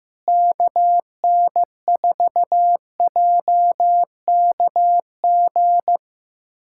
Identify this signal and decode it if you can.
Morse code, heard as a keyed tone, about 15 wpm, 700 Hz. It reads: KN4JKG
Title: USA Callsign DE Callsign 15wpm